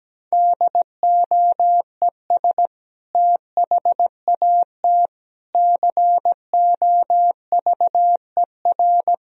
Morse code, heard as a keyed tone, about 17 wpm, 700 Hz. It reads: DOES THAT COVER
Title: Sentences from Top 1000 Words - Review Part 1 - 17wpm